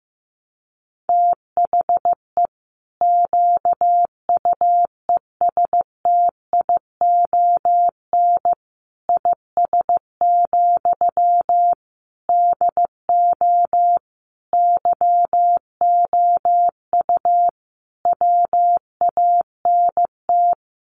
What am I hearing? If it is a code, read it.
Morse code, heard as a keyed tone, about 14 wpm, 700 Hz. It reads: THE QUESTION IS, DO YOU WANT
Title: Sentences from Top 1000 Words 15wpm